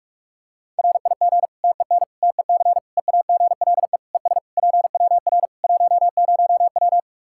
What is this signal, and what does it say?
Morse code, heard as a keyed tone, about 46 wpm, 700 Hz. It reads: RIG TEN TEC EAGLE ES PWR 10W
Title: K1USN SST QSO Practice 45wpm